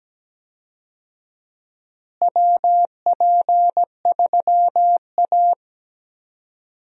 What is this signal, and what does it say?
Morse code, heard as a keyed tone, about 17 wpm, 700 Hz. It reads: WP3A
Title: USA Call Signs - 4 Characters - Repeated 2x - 17wpm